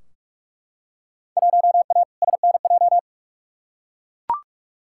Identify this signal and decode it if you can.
Morse code, heard as a keyed tone, about 41 wpm, 700 Hz. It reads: 1A SNJ
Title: Field Day Exchanges - 45wpm